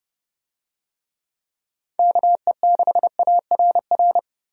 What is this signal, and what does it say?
Morse code, heard as a keyed tone, about 31 wpm, 700 Hz. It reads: KE6ARR
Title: USA Call Signs - 6 Characters - 30wpm